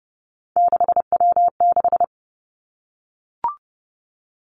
Morse code, heard as a keyed tone, about 30 wpm, 700 Hz. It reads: 6W6
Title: Three Challenging Letter-Number Combinations 30wpm